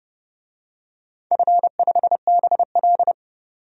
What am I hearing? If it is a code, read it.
Morse code, heard as a keyed tone, about 31 wpm, 700 Hz. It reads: F5BL
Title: DXCC-100 Call Signs - 4 Characters - 30wpm